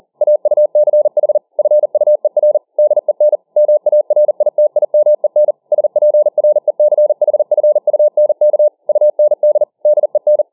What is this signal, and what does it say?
Morse code, heard as a keyed tone, about 40 wpm, 600 Hz. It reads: AUCH FUER DEN MARITIMEN SPRECHFUNK UND DEN